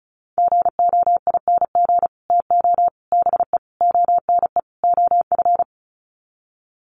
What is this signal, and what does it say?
Morse code, heard as a keyed tone, about 34 wpm, 700 Hz. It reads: GOING TO BE ONE OF
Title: Sets of 5 Words Encore2 35wpm